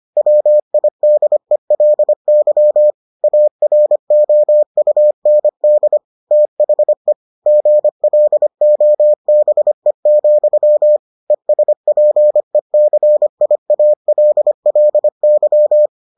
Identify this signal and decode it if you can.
Morse code, heard as a keyed tone, about 25 wpm, 600 Hz. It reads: WIDELY AROUND THE GLOBE, ESPECIALLY